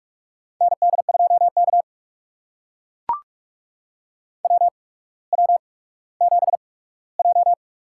Morse code, heard as a keyed tone, about 44 wpm, spaced as for 19 wpm, 700 Hz. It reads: ND2K WW7J
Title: USA Call Signs - 8x Character Spacing - 45wpm